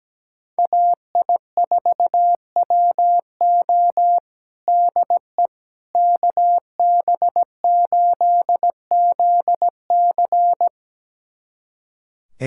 Morse code, heard as a keyed tone, about 17 wpm, 700 Hz. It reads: AI4WO DE KB8ZC
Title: USA Callsign DE Callsign - 5 Characters - 17wpm